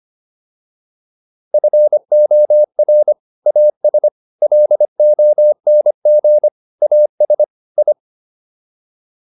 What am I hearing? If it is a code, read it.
Morse code, heard as a keyed tone, about 25 wpm, 600 Hz. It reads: FOR AS LONG AS I